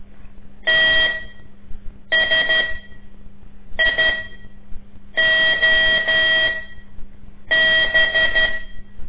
Morse code, heard as a keyed tone, about 12 wpm, spaced as for 6 wpm, 1950 Hz. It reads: TSIOB
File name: ik1weg2 / CORSI CW / lez5-2